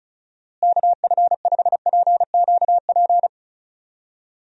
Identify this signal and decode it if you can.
Morse code, heard as a keyed tone, about 34 wpm, 700 Hz. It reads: KF5PQP
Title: USA Call Signs - 6 Characters - Repeated 2x - 35wpm